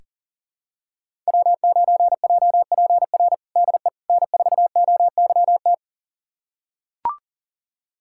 Morse code, heard as a keyed tone, about 40 wpm, 700 Hz. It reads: W9JPR DE N4OYT
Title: USA Callsign DE Callsign - 5 Characters - 40wpm